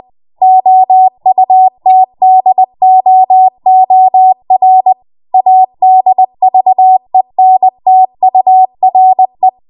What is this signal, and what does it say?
Morse code, heard as a keyed tone, about 20 wpm, 750 Hz. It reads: OUTDOOR ADVENTURE
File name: Morse / 20WPM